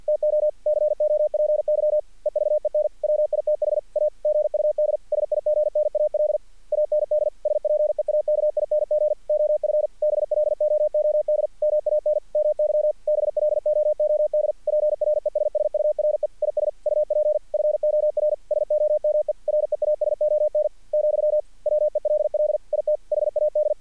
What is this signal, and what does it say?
Morse code, heard as a keyed tone, about 50 wpm, 600 Hz. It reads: TO COPY EVEN WITH A GUD SIGNAL AND SPEAKING OF BLOOD MAN MY BLOOD PRESSURE IS UP FOR SOME REASON , WELL IT HAD
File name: archived / CRSnet